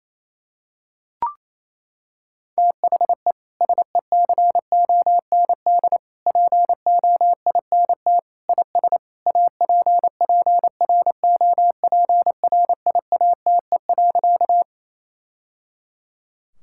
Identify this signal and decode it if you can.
Morse code, heard as a keyed tone, about 27 wpm, 700 Hz. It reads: THE SECOND POINT IS APPROPRIATE.
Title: Sentences from Top 1000 Words - Review Part 5 - 28wpm